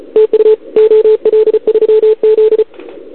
Morse code, heard as a keyed tone, about 34 wpm, 440 Hz. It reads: TU OL3Z
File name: CQWWCW2021 / OL3Z